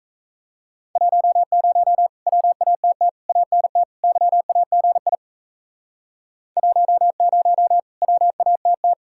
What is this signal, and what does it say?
Morse code, heard as a keyed tone, about 37 wpm, 700 Hz. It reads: 10 WATT ANT YAGI 10 WATT